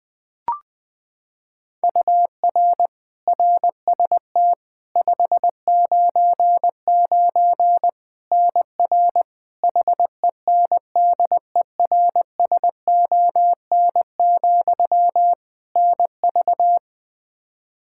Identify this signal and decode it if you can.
Morse code, heard as a keyed tone, about 20 wpm, 700 Hz. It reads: UR RST 599 NR HENDERSON, NV
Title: Practice QSO - Exchange 1 - Beginning - 20wpm